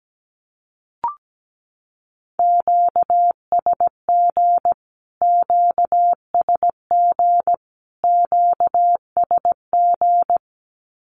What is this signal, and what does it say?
Morse code, heard as a keyed tone, about 17 wpm, 700 Hz. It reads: QSG QSG QSG
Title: Q-Codes Repeated 3x 17wpm